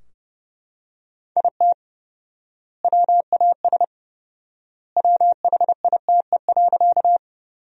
Morse code, heard as a keyed tone, about 32 wpm, 700 Hz. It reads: IT WAS WHITE.
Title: 4x Word-Spacing - Sentences from Top 300 Words - 30wpm